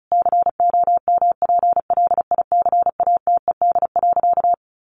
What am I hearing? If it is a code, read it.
Morse code, heard as a keyed tone, about 34 wpm, 700 Hz. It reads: COMPLICATED.
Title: Sentences from Top 1200 Words 35wpm